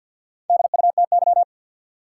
Morse code, heard as a keyed tone, about 49 wpm, 700 Hz. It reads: DUTY
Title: Sentences from Top 1500 Words - New Word Drill Alt - 50wpm